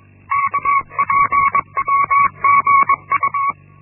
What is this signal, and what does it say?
Morse code, heard as a keyed tone, about 22 wpm, 1120 Hz. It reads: KPMGU
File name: Linksmakalnis-1